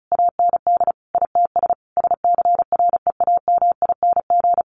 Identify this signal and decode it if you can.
Morse code, heard as a keyed tone, about 34 wpm, 700 Hz. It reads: AND ITS SCREAMING
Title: A Sound Of Thunder 35wpm